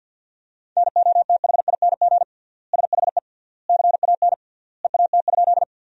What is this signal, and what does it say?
Morse code, heard as a keyed tone, about 50 wpm, 700 Hz. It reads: NOTHING SHE CAN EAT?
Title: 3x Word-Spacing - Sentences from Top 500 Words - 50wpm